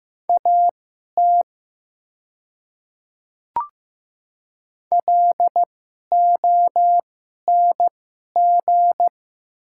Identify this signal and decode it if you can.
Morse code, heard as a keyed tone, about 15 wpm, spaced as for 11 wpm, 700 Hz. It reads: AT LONG
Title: 2x Character Spacing - Top 100 Words - 15wpm